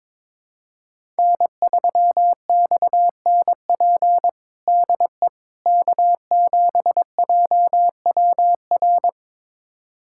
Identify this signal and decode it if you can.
Morse code, heard as a keyed tone, about 21 wpm, 700 Hz. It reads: N3XNP DE K7JWR
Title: USA Callsign DE Callsign - 5 Characters - 22wpm